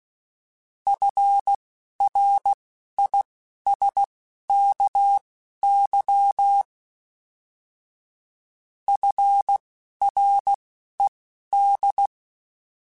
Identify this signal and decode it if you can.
Morse code, heard as a keyed tone, about 16 wpm, spaced as for 11 wpm, 800 Hz. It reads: FRISKY FRED
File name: F Runs / F Runs mp3s / F Runs @10